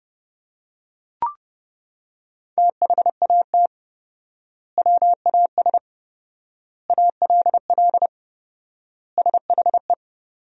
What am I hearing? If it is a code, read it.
Morse code, heard as a keyed tone, about 32 wpm, 700 Hz. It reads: THAT WAS ALL SHE
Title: 4x Word-Spacing - Sentences from Top 100 Words - 30wpm